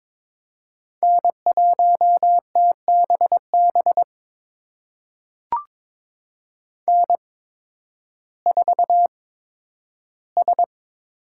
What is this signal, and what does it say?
Morse code, heard as a keyed tone, about 22 wpm, spaced as for 12 wpm, 700 Hz. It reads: N1TBB N4S
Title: USA Call Signs - 5 Characters - 8x Character Spacing - 22wpm